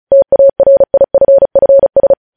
Morse code, heard as a keyed tone, about 36 wpm, 550 Hz. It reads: TARIFFS
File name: podcast / News headlines-1x-35x35